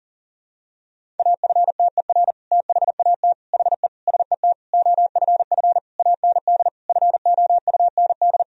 Morse code, heard as a keyed tone, about 40 wpm, 700 Hz. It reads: AFTER THAT HE SET OFF AND FOUND
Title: Sentences from Top 1000 Words - Review Part 1 - 40wpm